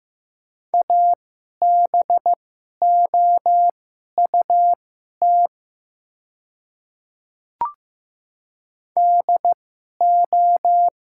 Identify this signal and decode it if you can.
Morse code, heard as a keyed tone, about 15 wpm, spaced as for 11 wpm, 700 Hz. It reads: ABOUT DO
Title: 2x Character Spacing - Top 100 Words - 15wpm